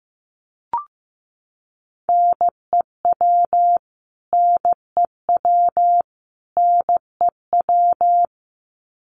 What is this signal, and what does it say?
Morse code, heard as a keyed tone, about 15 wpm, 700 Hz. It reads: NEW NEW NEW
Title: Top 200 Words Repeated 15wpm